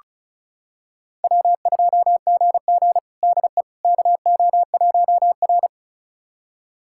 Morse code, heard as a keyed tone, about 34 wpm, 700 Hz. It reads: W2GG DE KO1R
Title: USA Callsign DE Callsign - 4 Characters - 35wpm